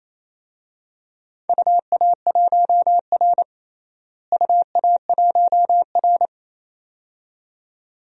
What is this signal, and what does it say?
Morse code, heard as a keyed tone, about 27 wpm, 700 Hz. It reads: UA1R UA1R
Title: DXCC-100 Call Signs - 4 Characters - Repeated 2x - 28wpm